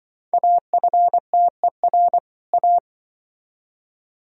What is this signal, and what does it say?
Morse code, heard as a keyed tone, about 25 wpm, 700 Hz. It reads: AFTER A
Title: Speed-Racing - Sets of 2 Words 20wpm